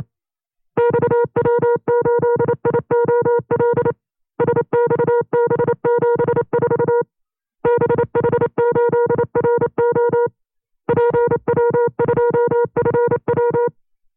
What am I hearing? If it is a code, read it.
Morse code, heard as a keyed tone, about 28 wpm, 480 Hz. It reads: XW8IOL SXB74 BH8RO PW2FW
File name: 25WPM